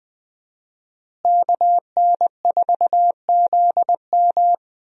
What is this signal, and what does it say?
Morse code, heard as a keyed tone, about 20 wpm, 700 Hz. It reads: KN4ZM
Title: USA Call Signs - 5 Characters - Repeated 2x - 20wpm